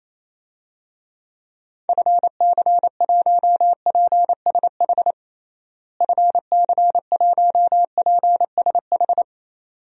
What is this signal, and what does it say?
Morse code, heard as a keyed tone, about 27 wpm, 700 Hz. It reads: FC1PSH FC1PSH
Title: DXCC-100 Call Signs - 6 Characters - Repeated 2x - 28wpm